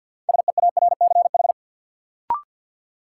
Morse code, heard as a keyed tone, about 53 wpm, 700 Hz. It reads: SEARCH
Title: Sentences from Top 800 Words - New Word Drill - 50wpm